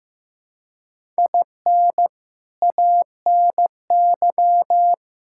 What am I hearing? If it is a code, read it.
Morse code, heard as a keyed tone, about 15 wpm, 700 Hz. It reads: IN ANY
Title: Sentences from Top 1000 Words - Review Part 6 - 15wpm